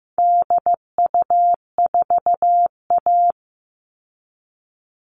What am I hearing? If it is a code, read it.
Morse code, heard as a keyed tone, about 15 wpm, 700 Hz. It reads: DU4A
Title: All Cut Numbers - 4-Digit 15wpm